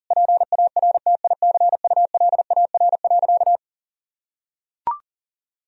Morse code, heard as a keyed tone, about 40 wpm, 700 Hz. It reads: PARTICULAR.
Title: Sentences from Top 1000 Words - Review Part 4 - 40wpm